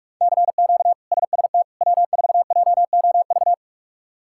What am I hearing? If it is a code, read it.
Morse code, heard as a keyed tone, about 46 wpm, 700 Hz. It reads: CQ SST W4JOV
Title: K1USN SST QSO Practice - 45wpm